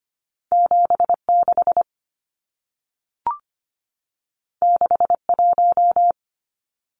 Morse code, heard as a keyed tone, about 25 wpm, 700 Hz. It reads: 76 61
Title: Numbers - 2-Digit 25wpm